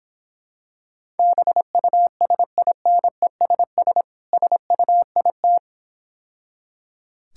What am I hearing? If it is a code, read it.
Morse code, heard as a keyed tone, about 26 wpm, 690 Hz. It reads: BUSINESS SUIT